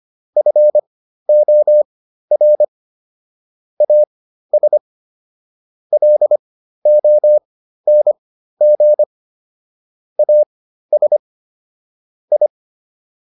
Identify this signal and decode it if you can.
Morse code, heard as a keyed tone, about 25 wpm, spaced as for 13 wpm, 600 Hz. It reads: FOR AS LONG AS I